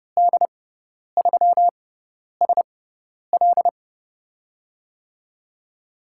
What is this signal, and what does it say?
Morse code, heard as a keyed tone, about 30 wpm, spaced as for 10 wpm, 700 Hz. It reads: D3SL
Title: USA Call Signs - 5 Characters - 6x Character Spacing - 30wpm